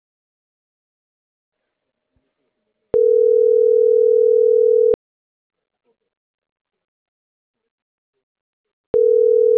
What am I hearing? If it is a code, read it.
Ringback tone (North America), 440 + 480 Hz.